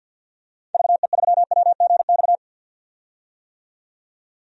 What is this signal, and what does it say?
Morse code, heard as a keyed tone, about 50 wpm, 700 Hz. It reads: VE3WGX
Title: DXCC-100 Call Signs - 6 Characters - 50wpm